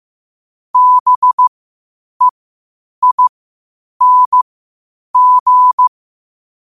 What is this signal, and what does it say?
Morse code, heard as a keyed tone, about 15 wpm, spaced as for 8 wpm, 1000 Hz. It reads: BEING
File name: qotdfiles / 8wpm